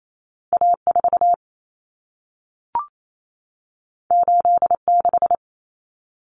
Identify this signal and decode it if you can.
Morse code, heard as a keyed tone, about 28 wpm, 700 Hz. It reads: A4 86
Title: Common Cut Numbers - 2-Digit 28wpm